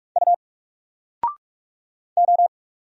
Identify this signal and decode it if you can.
Morse code, heard as a keyed tone, about 46 wpm, 700 Hz. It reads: U O
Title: Single Letter-Number 45wpm